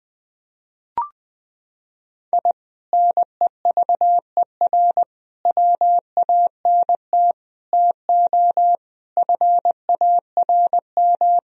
Morse code, heard as a keyed tone, about 20 wpm, 700 Hz. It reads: I NEVER WANT TO FARM
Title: Sentences from Top 1000 Words - Review Part 2 - 20wpm